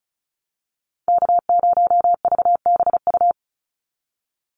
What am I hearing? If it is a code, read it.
Morse code, heard as a keyed tone, about 34 wpm, 700 Hz. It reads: K0VBU
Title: Call Signs - 35wpm